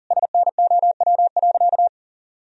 Morse code, heard as a keyed tone, about 40 wpm, 700 Hz. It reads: SNOW.
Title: Sentences from Top 1000 Words - Review Part 3 - 40wpm